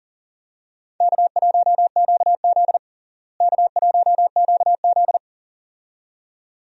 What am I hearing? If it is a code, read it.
Morse code, heard as a keyed tone, about 40 wpm, 700 Hz. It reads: K1QZ K1QZ
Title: USA Call Signs - 4 Characters - Repeated 2x - 40wpm